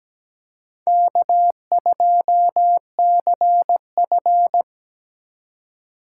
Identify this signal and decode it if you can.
Morse code, heard as a keyed tone, about 17 wpm, 700 Hz. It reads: K2CF
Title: USA Call Signs - 4 Characters - 6x Character Spacing - 17wpm